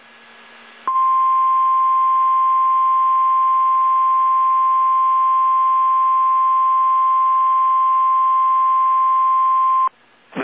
Emergency Alert System (SAME) tones (the 1050 Hz attention tone). It is heard from 0.8 to 9.9 s.
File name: eas / KSDZKDJL / KSDZKDJL